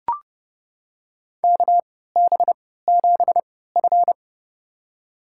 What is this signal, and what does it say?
Morse code, heard as a keyed tone, about 32 wpm, spaced as for 17 wpm, 700 Hz. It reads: KB7F
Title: USA Call Signs - 4 Characters - 3x Character Spacing - 30wpm